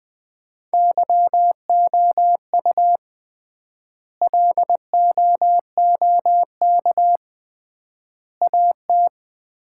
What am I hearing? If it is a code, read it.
Morse code, heard as a keyed tone, about 21 wpm, 700 Hz. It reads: YOU LOOK AT
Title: 3x Word-Spacing - Sets of 3 Words - 20wpm